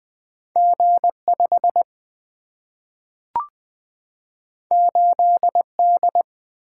Morse code, heard as a keyed tone, about 20 wpm, 700 Hz. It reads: G 5 8 D
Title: Single-Letter Single-Number 20wpm